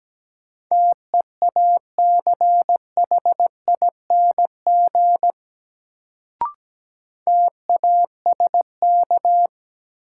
Morse code, heard as a keyed tone, about 17 wpm, 700 Hz. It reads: TEACHING TASK
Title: Sentences from Top 1100 Words - New Word Drill Alt - 17wpm